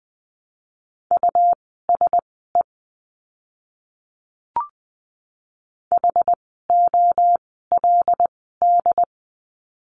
Morse code, heard as a keyed tone, about 20 wpm, spaced as for 14 wpm, 700 Hz. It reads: USE HOLD
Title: 101 Lesson 17f - Letter u - Words 20wpm